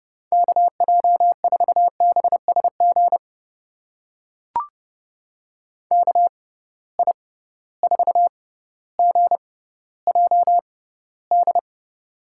Morse code, heard as a keyed tone, about 30 wpm, spaced as for 15 wpm, 700 Hz. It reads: KJ4BSG KI4GJD
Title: USA Call Signs - 6 Characters - 6x Character Spacing - 30wpm